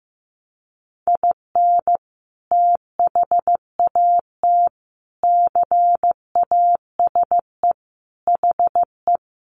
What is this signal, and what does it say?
Morse code, heard as a keyed tone, about 15 wpm, 700 Hz. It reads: IN THAT CASE HE
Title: Sentences from Top 800 Words 15wpm